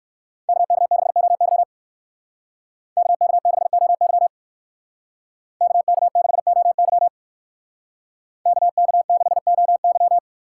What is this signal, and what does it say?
Morse code, heard as a keyed tone, about 64 wpm, 700 Hz. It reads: KK6OY KK6OY KK6OY KK6OY